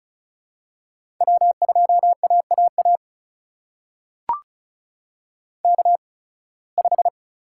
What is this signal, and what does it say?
Morse code, heard as a keyed tone, about 36 wpm, spaced as for 23 wpm, 700 Hz. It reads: W2AAA K5
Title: USA Call Signs - 5 Characters - 8x Character Spacing - 35wpm